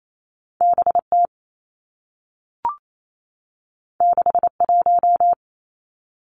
Morse code, heard as a keyed tone, about 28 wpm, 700 Hz. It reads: B T 6 1
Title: Two Challenging Letter-Number Combinations 28wpm